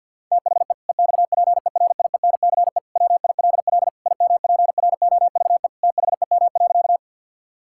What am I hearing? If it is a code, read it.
Morse code, heard as a keyed tone, about 50 wpm, 700 Hz. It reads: THE EXPERIENCE WILL IMPROVE THEM.
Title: Sentences from Top 1000 Words - Review Part 5 - 50wpm